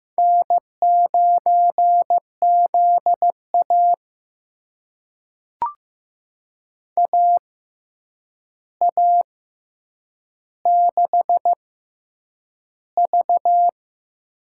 Morse code, heard as a keyed tone, about 15 wpm, spaced as for 8 wpm, 700 Hz. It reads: N9ZA AA6V
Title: USA Call Signs - 4 Characters - 6x Character Spacing - 15wpm